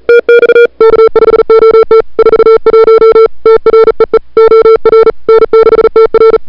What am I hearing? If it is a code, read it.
Morse code, heard as a keyed tone, about 36 wpm, 440 Hz. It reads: TX K5OT 41 TREE OR N6TR
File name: CWSprint / February 2007 CW Sprint / 01-2007 CW Sprint Hour1